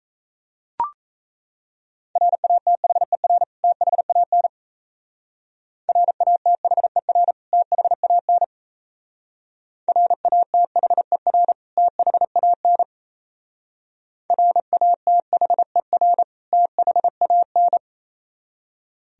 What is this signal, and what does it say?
Morse code, heard as a keyed tone, about 34 wpm, 690 Hz. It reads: RATHER THAN RATHER THAN RATHER THAN RATHER THAN
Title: Speed-Racing - Sets of 2 Words 28wpm